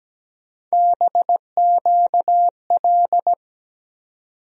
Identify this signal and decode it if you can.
Morse code, heard as a keyed tone, about 17 wpm, 700 Hz. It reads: BQL
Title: USA Call Sign Postfixes - 3 Letters 17wpm